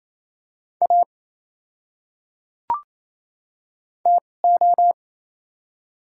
Morse code, heard as a keyed tone, about 28 wpm, spaced as for 3 wpm, 700 Hz. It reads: AT O
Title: 2x Character Spacing - Top 100 Words - 28wpm